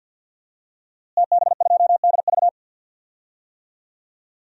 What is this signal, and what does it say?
Morse code, heard as a keyed tone, about 52 wpm, 700 Hz. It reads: TB2DV
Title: DXCC-100 Call Signs - 5 Characters - Repeated 2x - 50wpm